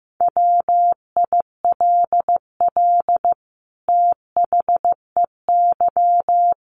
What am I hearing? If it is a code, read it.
Morse code, heard as a keyed tone, about 15 wpm, 700 Hz. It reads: WILL THEY
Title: Sentences from Top 1000 Words 15wpm